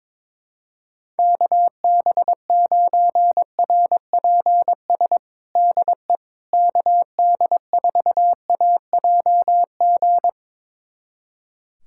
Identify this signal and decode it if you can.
Morse code, heard as a keyed tone, about 22 wpm, 700 Hz. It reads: KB9RPS DE KD4AJG
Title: USA Callsign DE Callsign - 6 Characters - 22wpm